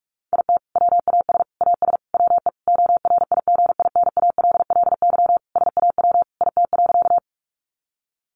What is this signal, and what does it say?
Morse code, heard as a keyed tone, about 44 wpm, 700 Hz. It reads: IT WAS AS WE ORIGINALLY SAW IT.
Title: Sentences from Top 1500 Words 45wpm